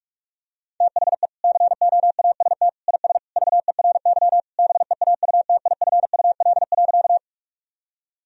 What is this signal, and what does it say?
Morse code, heard as a keyed tone, about 45 wpm, 700 Hz. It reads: THE COAST IS VERY BEAUTIFUL.
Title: Sentences from Top 1000 Words - Review Part 4 - 45wpm